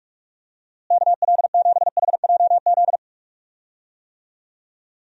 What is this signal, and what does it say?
Morse code, heard as a keyed tone, about 46 wpm, 700 Hz. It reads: KL7HJZ
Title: DXCC-100 Call Signs - 6 Characters - Repeated 2x - 45wpm